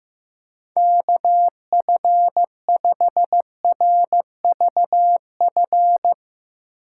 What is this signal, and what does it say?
Morse code, heard as a keyed tone, about 15 wpm, 700 Hz. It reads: KF5RVF
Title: USA Call Signs - 6 Characters - 3x Character Spacing - 15wpm